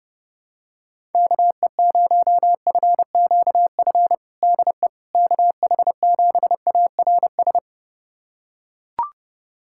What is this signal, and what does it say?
Morse code, heard as a keyed tone, about 31 wpm, 700 Hz. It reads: KE0FQF DE KH7ARS
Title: USA Callsign DE Callsign - 6 Characters - 30wpm